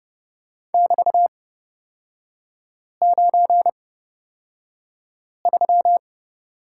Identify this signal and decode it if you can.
Morse code, heard as a keyed tone, about 30 wpm, spaced as for 5 wpm, 700 Hz. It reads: <BT>93
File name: Lesson 36e - BT - Three Characters 30wpm